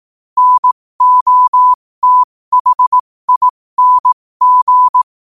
Morse code, heard as a keyed tone, about 18 wpm, spaced as for 15 wpm, 1000 Hz. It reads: NOTHING
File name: qotdfiles / 15wpm